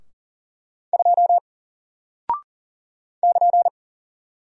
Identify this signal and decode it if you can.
Morse code, heard as a keyed tone, about 39 wpm, 700 Hz. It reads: 2 <KN>
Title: Single Character (Letter:Number:Punctuation:Prosign) Alt 40wpm